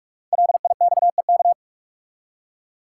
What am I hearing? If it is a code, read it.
Morse code, heard as a keyed tone, about 44 wpm, 700 Hz. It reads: LIXEK
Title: Code Groups - 5 Letter:Numbers - 45wpm